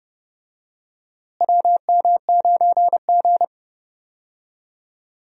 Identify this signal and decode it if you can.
Morse code, heard as a keyed tone, about 29 wpm, 700 Hz. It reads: WM9G
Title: USA Call Signs - 4 Characters - Repeated 3x - 30wpm